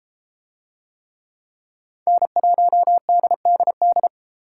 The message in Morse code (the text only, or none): N1DDD